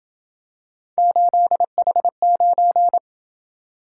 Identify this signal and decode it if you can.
Morse code, heard as a keyed tone, about 28 wpm, 700 Hz. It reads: 8H9